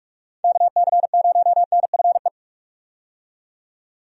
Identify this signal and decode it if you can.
Morse code, heard as a keyed tone, about 46 wpm, 700 Hz. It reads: KC0NFE
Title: USA Call Signs - 6 Characters - 45wpm